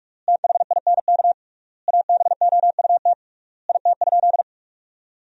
Morse code, heard as a keyed tone, about 43 wpm, 700 Hz. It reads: THINK ABOUT IT?
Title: 3x Word-Spacing - Sentences from Top 200 Words - 45wpm